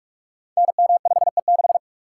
Morse code, heard as a keyed tone, about 44 wpm, 700 Hz. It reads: NM5E6
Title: Code Groups - 5 Letter:Numbers - 45wpm